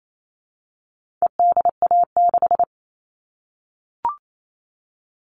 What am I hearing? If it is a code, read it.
Morse code, heard as a keyed tone, about 28 wpm, 700 Hz. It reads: EDA6
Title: All Cut Numbers - 4-Digit 28wpm